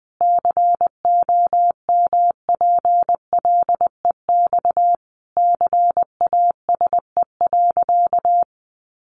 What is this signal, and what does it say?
Morse code, heard as a keyed tone, about 20 wpm, 700 Hz. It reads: COMPLEX CASE.
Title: Sentences from Top 900 Words 20wpm